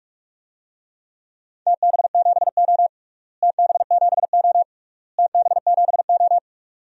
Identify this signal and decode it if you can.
Morse code, heard as a keyed tone, about 41 wpm, 700 Hz. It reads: TB7O TB7O TB7O
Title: DXCC-100 Call Signs - 4 Characters - Repeated 3x - 45wpm